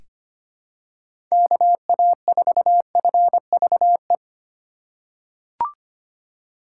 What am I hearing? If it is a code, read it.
Morse code, heard as a keyed tone, about 25 wpm, 700 Hz. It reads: KA4FVE
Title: USA Call Signs - 6 Characters - 4x Character Spacing - 25wpm